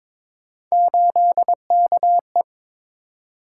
Morse code, heard as a keyed tone, about 22 wpm, 700 Hz. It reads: 8KE
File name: practice / 22wpm / Three Letter-Number Combinations - 22wpm